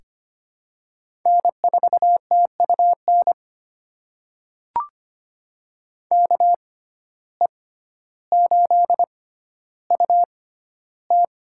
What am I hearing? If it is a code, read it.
Morse code, heard as a keyed tone, about 25 wpm, spaced as for 13 wpm, 700 Hz. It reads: N4TUN KE8UT
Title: USA Call Signs - 5 Characters - 6x Character Spacing - 25wpm